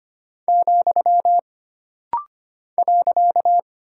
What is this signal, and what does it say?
Morse code, heard as a keyed tone, about 25 wpm, 700 Hz. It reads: , .
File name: Single Punctuation-Prosign - Rapid-Fire - 25wpm